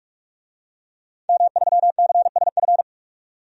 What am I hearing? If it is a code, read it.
Morse code, heard as a keyed tone, about 45 wpm, 700 Hz. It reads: M3CSF
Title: DXCC-100 Call Signs - 5 Characters - 45wpm